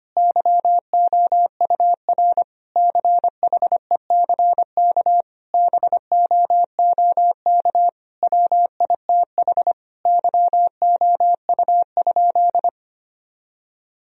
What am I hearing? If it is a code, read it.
Morse code, heard as a keyed tone, about 25 wpm, 700 Hz. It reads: YOUR CHECK BOOK WITH YOU?
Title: Sentences from Top 1000 Words - Review Part 3 - 25wpm